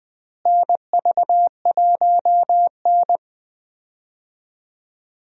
Morse code, heard as a keyed tone, about 21 wpm, 700 Hz. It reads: NV1N
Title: USA Call Signs - 4 Characters - Repeated 3x - 20wpm